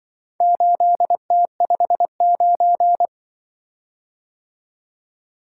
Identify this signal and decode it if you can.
Morse code, heard as a keyed tone, about 25 wpm, 700 Hz. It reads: 8T59
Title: Speed-Racing - Four Letter-Number Combinations - 20wpm